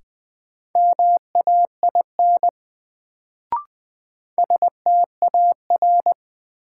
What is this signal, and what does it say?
Morse code, heard as a keyed tone, about 20 wpm, 700 Hz. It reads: MAIN STAR
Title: Sentences from Top 1000 Words - Review Part 2 - New Word Drill - 20wpm